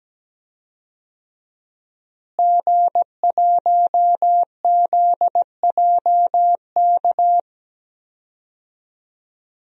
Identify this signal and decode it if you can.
Morse code, heard as a keyed tone, about 17 wpm, 700 Hz. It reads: G1ZJK
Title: DXCC-100 Call Signs - 5 Characters - Repeated 3x - 17wpm